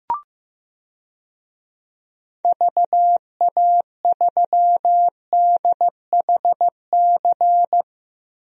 Morse code, heard as a keyed tone, about 16 wpm, 700 Hz. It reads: VA3DHC
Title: DXCC-100 Call Signs - 6 Characters - Repeated 2x - 15wpm